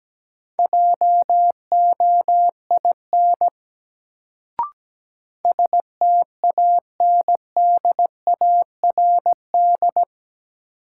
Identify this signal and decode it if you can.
Morse code, heard as a keyed tone, about 17 wpm, 700 Hz. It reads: JOIN STANDARD
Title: Sentences from Top 1000 Words - Review Part 6 - New Word Drill - 17wpm